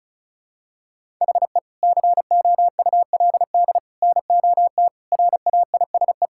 Morse code, heard as a keyed tone, about 34 wpm, 700 Hz. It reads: HE COULD NOT RAISE
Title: Sentences from Top 1600 Words 35wpm